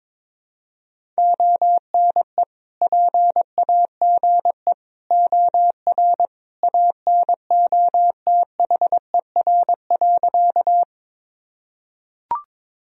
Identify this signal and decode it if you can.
Morse code, heard as a keyed tone, about 21 wpm, 700 Hz. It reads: ONE PAGE OR ANOTHER.
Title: Sentences from Top 1000 Words - Review Part 1 - 22wpm